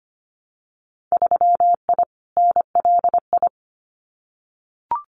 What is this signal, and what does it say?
Morse code, heard as a keyed tone, about 25 wpm, 700 Hz. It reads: 3I NLI
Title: Winter Field Day Exchanges Encore 25wpm